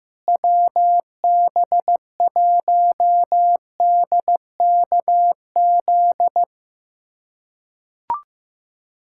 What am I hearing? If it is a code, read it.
Morse code, heard as a keyed tone, about 15 wpm, 700 Hz. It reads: WB1DKZ
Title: USA Callsign DE Callsign 15wpm